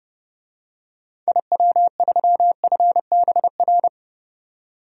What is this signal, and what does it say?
Morse code, heard as a keyed tone, about 31 wpm, 700 Hz. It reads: IW3FBR
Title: DXCC-100 Call Signs - 6 Characters - 30wpm